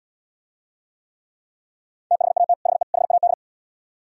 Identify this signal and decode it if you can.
Morse code, heard as a keyed tone, about 75 wpm, 700 Hz. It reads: THAT HE HAD